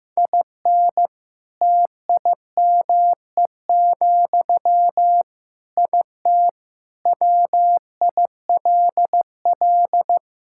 Morse code, heard as a keyed tone, about 15 wpm, 700 Hz. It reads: IN TIME, IT WILL
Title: Sentences from Top 1600 Words 15wpm